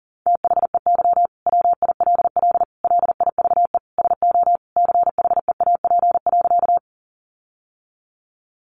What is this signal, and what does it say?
Morse code, heard as a keyed tone, about 40 wpm, 700 Hz. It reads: THEY WILL LIVE SO CHEAP.
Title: Sentences from Top 1200 Words 40wpm